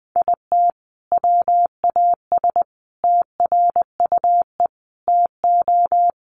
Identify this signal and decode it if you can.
Morse code, heard as a keyed tone, about 20 wpm, 700 Hz. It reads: IT WAS TRUE TO
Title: Sentences from Top 400 Words 20wpm